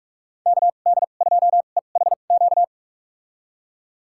Morse code, heard as a keyed tone, about 44 wpm, spaced as for 32 wpm, 700 Hz. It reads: KD2EHQ
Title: USA Call Signs - 6 Characters - 2x Character Spacing - 45wpm